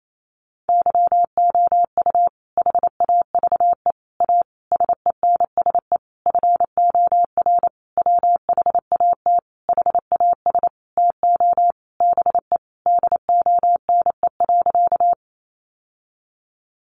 Morse code, heard as a keyed tone, about 28 wpm, 700 Hz. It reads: YOU HAVE A SENSE FOR WHAT HAS TO BE DONE.
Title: Sentences from Top 700 Words 28wpm